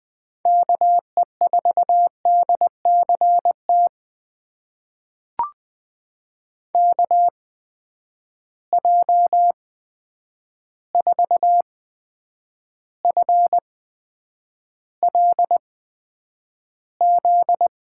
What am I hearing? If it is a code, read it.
Morse code, heard as a keyed tone, about 20 wpm, spaced as for 9 wpm, 700 Hz. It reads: KE4DCT KJ4FLZ
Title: USA Call Signs - 6 Characters - 8x Character Spacing - 20wpm